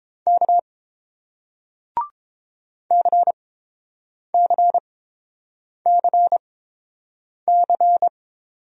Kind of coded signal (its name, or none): Morse code